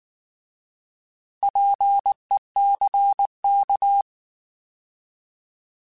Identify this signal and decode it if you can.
Morse code, heard as a keyed tone, about 19 wpm, 800 Hz. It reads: PECK
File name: Alphabetical / Alphabetical mp3s / Ps @18